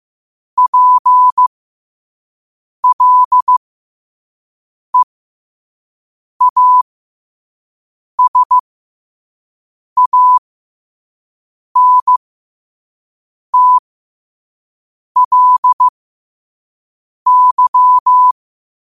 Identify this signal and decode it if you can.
Morse code, heard as a keyed tone, about 15 wpm, spaced as for 5 wpm, 1000 Hz. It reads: PLEASANTLY
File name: qotdfiles / 5wpm